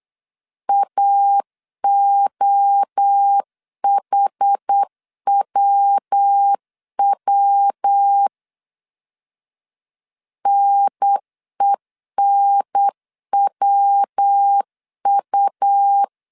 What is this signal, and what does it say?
Morse code, heard as a keyed tone, about 8 wpm, 800 Hz. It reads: AOHWW NENWU
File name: CW / DET / 1 3 1500C S B 8wpm 4141